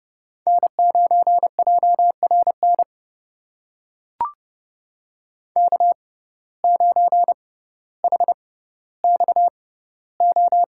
Morse code, heard as a keyed tone, about 30 wpm, spaced as for 15 wpm, 700 Hz. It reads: N9JRN K9HXO
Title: USA Call Signs - 5 Characters - 6x Character Spacing - 30wpm